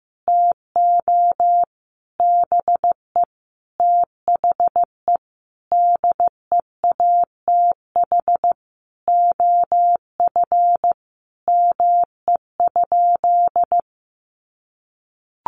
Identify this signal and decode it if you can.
Morse code, heard as a keyed tone, about 15 wpm, 700 Hz. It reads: TO BE THE DEATH OF ME?
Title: Sentences from Top 1100 Words 15wpm